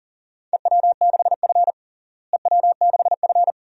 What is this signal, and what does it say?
Morse code, heard as a keyed tone, about 37 wpm, 700 Hz. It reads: EW6F EW6F
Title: DXCC-100 Call Signs - 4 Characters - Repeated 2x - 40wpm